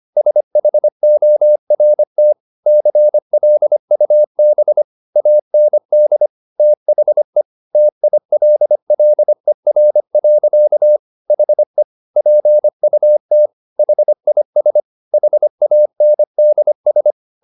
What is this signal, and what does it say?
Morse code, heard as a keyed tone, about 25 wpm, 600 Hz. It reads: SHORT CLUB AND THE TILLER. HE PUT HIS HANDS